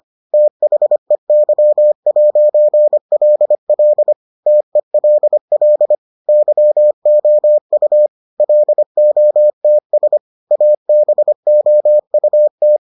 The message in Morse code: THEY'LL TELL YOU LOTS ABOUT

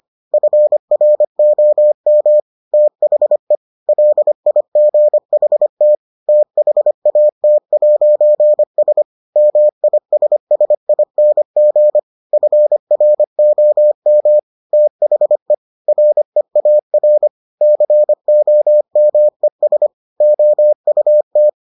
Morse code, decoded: FROM THE LIGHT THAT'S MISSING FROM THE REAR COMES OUT

